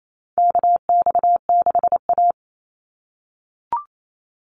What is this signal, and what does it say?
Morse code, heard as a keyed tone, about 28 wpm, 700 Hz. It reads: KX6A
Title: NA SOTA Chaser Callsigns 28wpm